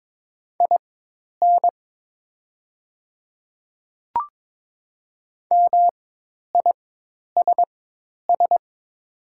Morse code, heard as a keyed tone, about 22 wpm, spaced as for 10 wpm, 700 Hz. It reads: IN MISS
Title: 4x Character Spacing - Top 500 Words - 22wpm